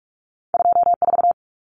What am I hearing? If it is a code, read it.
Morse code, heard as a keyed tone, about 46 wpm, 700 Hz. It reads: 24
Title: Numbers - 2-Digit 45wpm